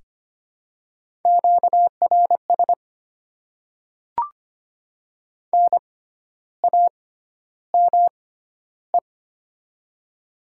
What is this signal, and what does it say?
Morse code, heard as a keyed tone, about 25 wpm, spaced as for 12 wpm, 700 Hz. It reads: QRS NAME
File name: Top 90 QSO Elements - 6x Character Spacing - 25wpm